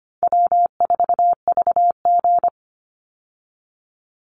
Morse code, heard as a keyed tone, about 25 wpm, 700 Hz. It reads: W4VG
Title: Call Signs Repeated 3x - 25wpm